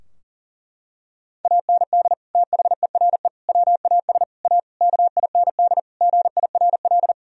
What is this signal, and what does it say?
Morse code, heard as a keyed tone, about 40 wpm, 700 Hz. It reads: AND THERE WAS A KIND GIRL
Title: Sentences from Top 1000 Words - Review Part 3 - 40wpm